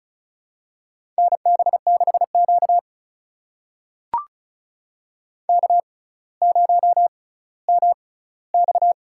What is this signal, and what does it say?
Morse code, heard as a keyed tone, about 36 wpm, spaced as for 18 wpm, 700 Hz. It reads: NB6Q K0MX
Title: USA Call Signs - 4 Characters - 6x Character Spacing - 35wpm